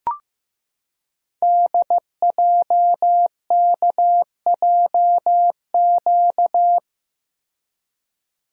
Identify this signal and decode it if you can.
Morse code, heard as a keyed tone, about 15 wpm, 700 Hz. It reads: DJKJQ
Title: Code Groups - 5 Letters - 15wpm